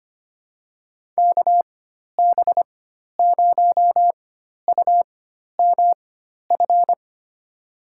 Morse code, heard as a keyed tone, about 25 wpm, spaced as for 12 wpm, 700 Hz. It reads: KB0UMF
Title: USA Call Signs - 6 Characters - 4x Character Spacing - 25wpm